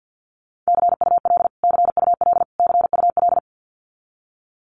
Morse code, heard as a keyed tone, about 50 wpm, 700 Hz. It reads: CUL CUL CUL
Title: Abbreviations Repeated 3x - 50wpm